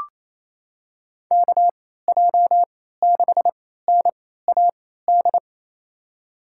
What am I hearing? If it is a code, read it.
Morse code, heard as a keyed tone, about 28 wpm, spaced as for 16 wpm, 700 Hz. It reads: KJ6NAD
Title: USA Call Signs - 6 Characters - 3x Character Spacing - 28wpm